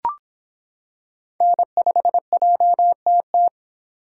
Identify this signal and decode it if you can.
Morse code, heard as a keyed tone, about 25 wpm, 700 Hz. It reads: N5JTT